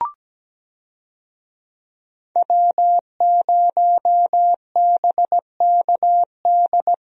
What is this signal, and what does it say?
Morse code, heard as a keyed tone, about 17 wpm, 700 Hz. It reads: W0BKD
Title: USA Call Signs - 5 Characters - Repeated 2x - 17wpm